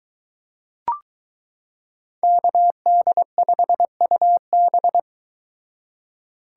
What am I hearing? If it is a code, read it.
Morse code, heard as a keyed tone, about 22 wpm, 700 Hz. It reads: KD5UB